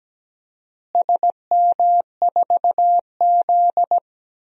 Morse code, heard as a keyed tone, about 17 wpm, 700 Hz. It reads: SM4Z
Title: DXCC-100 Call Signs - 4 Characters - Repeated 2x - 17wpm